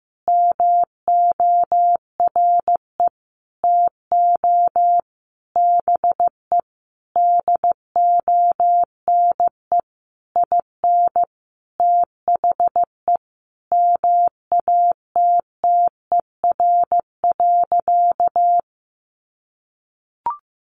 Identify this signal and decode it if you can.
Morse code, heard as a keyed tone, about 15 wpm, 700 Hz. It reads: MORE TO BE DONE IN THE MATTER.
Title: Sentences from Top 700 Words 15wpm